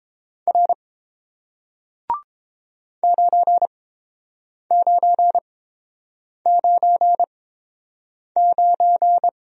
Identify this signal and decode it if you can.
Morse code, heard as a keyed tone, about 31 wpm, spaced as for 7 wpm, 700 Hz. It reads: R 9999